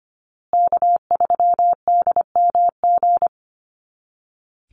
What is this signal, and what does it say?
Morse code, heard as a keyed tone, about 25 wpm, 700 Hz. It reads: K3DMG
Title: Call Signs Repeated 2x - 25wpm